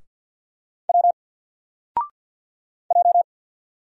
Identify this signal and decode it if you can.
Morse code, heard as a keyed tone, about 50 wpm, 700 Hz. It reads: W J
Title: Single Letter-Number 50wpm